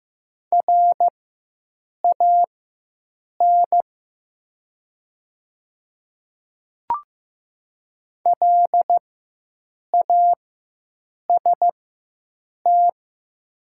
Morse code, heard as a keyed tone, about 15 wpm, spaced as for 7 wpm, 690 Hz. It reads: RAN LAST
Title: 4x Character Spacing - Top 500 Words - 15wpm